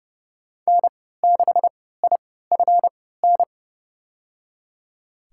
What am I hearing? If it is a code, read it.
Morse code, heard as a keyed tone, about 31 wpm, spaced as for 17 wpm, 700 Hz. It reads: N6IFN
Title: USA Call Signs - 5 Characters - 3x Character Spacing - 30wpm